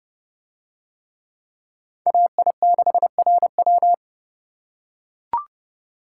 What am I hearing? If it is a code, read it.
Morse code, heard as a keyed tone, about 29 wpm, 700 Hz. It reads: AI6RW